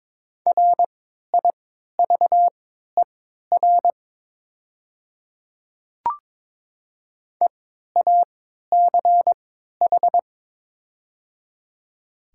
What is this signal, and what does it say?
Morse code, heard as a keyed tone, about 22 wpm, spaced as for 12 wpm, 700 Hz. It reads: RIVER EACH
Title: 3x Character Spacing - Top 500 Words - 22wpm